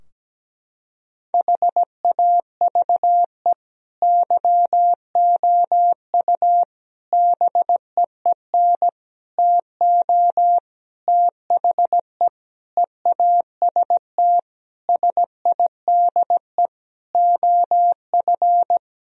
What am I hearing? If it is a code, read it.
Morse code, heard as a keyed tone, about 17 wpm, 700 Hz. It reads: HAVE YOU BEEN TO THE EAST SIDE OF
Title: Sentences from Top 1000 Words - Review Part 3 - 17wpm